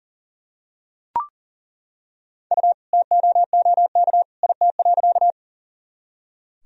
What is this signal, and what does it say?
Morse code, heard as a keyed tone, about 38 wpm, 700 Hz. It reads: U TOOK IT.
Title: Sentences from Top 300 Words and Abbreviations 40wpm